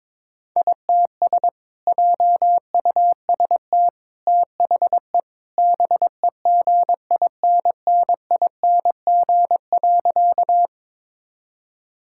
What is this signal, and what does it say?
Morse code, heard as a keyed tone, about 22 wpm, 700 Hz. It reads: ITS JUST THE BEGINNING.
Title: Sentences from Top 1000 Words - Review Part 5 - 22wpm